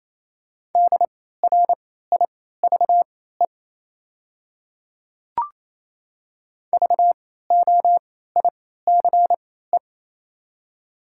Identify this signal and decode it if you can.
Morse code, heard as a keyed tone, about 28 wpm, spaced as for 16 wpm, 700 Hz. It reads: DRIVE VOICE
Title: 3x Character Spacing - Top 500 Words - 28wpm